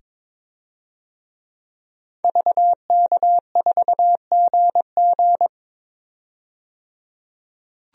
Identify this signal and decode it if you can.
Morse code, heard as a keyed tone, about 22 wpm, 700 Hz. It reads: VK4GG
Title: DXCC-100 Call Signs - 5 Characters - 22wpm